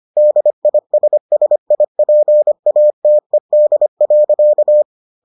Morse code, heard as a keyed tone, about 25 wpm, 600 Hz. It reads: DISSIPATED.